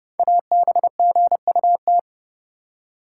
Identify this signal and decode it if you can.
Morse code, heard as a keyed tone, about 28 wpm, 700 Hz. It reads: ABGUT
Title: Code Groups - 5 Letters - 30wpm